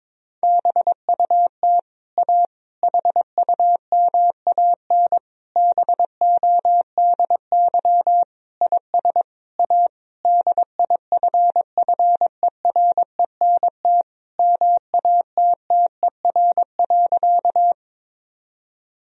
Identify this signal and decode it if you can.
Morse code, heard as a keyed tone, about 23 wpm, 690 Hz. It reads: BUT A HUMAN BODY IS A DIFFERENT MATTER.
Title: Sentences from Top 1000 Words - Review Part 3 - 22wpm